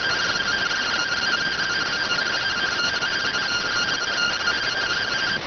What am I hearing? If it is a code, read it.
Two-tone teleprinter signal, about 170 Hz shift.